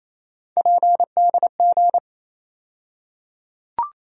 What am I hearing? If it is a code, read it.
Morse code, heard as a keyed tone, about 28 wpm, 700 Hz. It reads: PDG
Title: USA Call Sign Postfixes - 3 Letters 28wpm